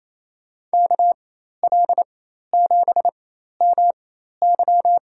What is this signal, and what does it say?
Morse code, heard as a keyed tone, about 27 wpm, spaced as for 13 wpm, 700 Hz. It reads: KL7MY
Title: USA Call Signs - 5 Characters - 4x Character Spacing - 28wpm